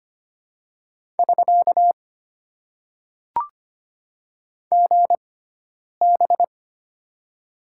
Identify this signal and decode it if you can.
Morse code, heard as a keyed tone, about 25 wpm, spaced as for 9 wpm, 700 Hz. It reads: <SK> GB
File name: Top 90 QSO Elements - 6x Character Spacing - 25wpm